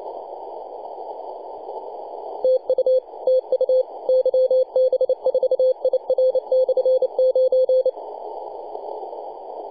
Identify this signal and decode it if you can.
Morse code, heard as a keyed tone, about 29 wpm, 520 Hz. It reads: TU TU YB4IR/9
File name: posts / 2015-june-21-0848z-yb4ir-9-14-010mhz-cw